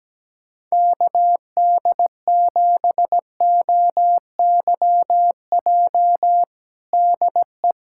Morse code, heard as a keyed tone, about 17 wpm, 700 Hz. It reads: KD7OYJ DE
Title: USA Callsign DE Callsign - 6 Characters - 17wpm